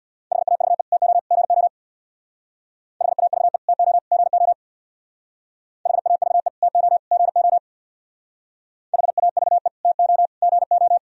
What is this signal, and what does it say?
Morse code, heard as a keyed tone, about 63 wpm, 700 Hz. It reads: HAVE TO GO HAVE TO GO HAVE TO GO HAVE TO GO